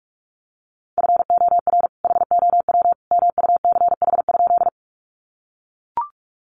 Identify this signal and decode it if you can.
Morse code, heard as a keyed tone, about 44 wpm, 700 Hz. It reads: FOR HOW MUCH?
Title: Sentences from Top 200 Words 45wpm